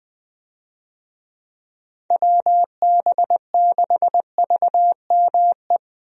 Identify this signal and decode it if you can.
Morse code, heard as a keyed tone, about 20 wpm, 700 Hz. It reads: WB6VME
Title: USA Call Signs - 6 Characters - Repeated 2x - 20wpm